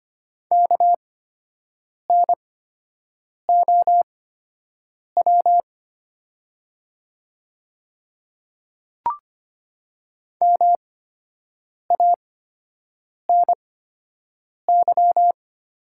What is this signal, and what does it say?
Morse code, heard as a keyed tone, about 25 wpm, spaced as for 7 wpm, 700 Hz. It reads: KNOW MANY
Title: 8x Character Spacing - Top 100 Words - 25wpm